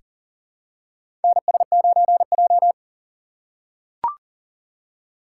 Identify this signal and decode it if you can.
Morse code, heard as a keyed tone, about 40 wpm, 700 Hz. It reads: NS9J
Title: USA Call Signs - 2x Character Spacing - 40wpm